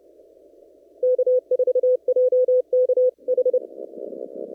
Morse code, heard as a keyed tone, about 30 wpm, 500 Hz. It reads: K4JKH
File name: CW / kt5j wpx cw-op-k5pi